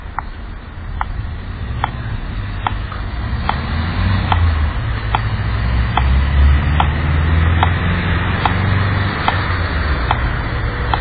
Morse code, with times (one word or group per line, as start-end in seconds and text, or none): none